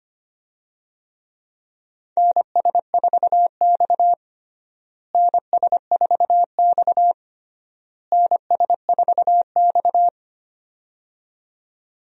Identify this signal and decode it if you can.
Morse code, heard as a keyed tone, about 25 wpm, 700 Hz. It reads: NS4X NS4X NS4X
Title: USA Call Signs - 4 Characters - Repeated 3x - 25wpm